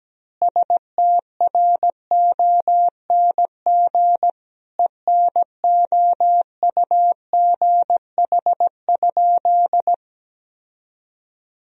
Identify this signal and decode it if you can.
Morse code, heard as a keyed tone, about 17 wpm, 700 Hz. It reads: STRONG ENOUGH?
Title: Sentences from Top 1000 Words - Review Part 3 - 17wpm